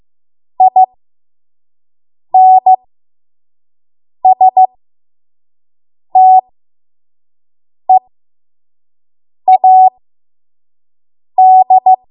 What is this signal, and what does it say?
Morse code, heard as a keyed tone, about 15 wpm, spaced as for 5 wpm, 750 Hz. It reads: INSTEAD
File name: Morse / 05WPM